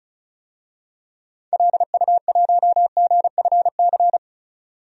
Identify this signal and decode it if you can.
Morse code, heard as a keyed tone, about 35 wpm, 700 Hz. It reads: LU1GFC
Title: DXCC-100 Call Signs - 6 Characters - 35wpm